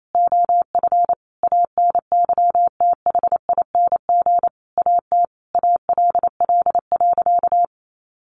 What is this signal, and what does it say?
Morse code, heard as a keyed tone, about 28 wpm, 700 Hz. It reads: OF ANYTHING AT ALL.
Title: Sentences from Top 1300 Words 28wpm